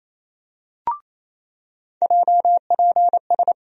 Morse code, heard as a keyed tone, about 28 wpm, 700 Hz. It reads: JPS
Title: USA Call Sign Postfixes - 3 Letters 28wpm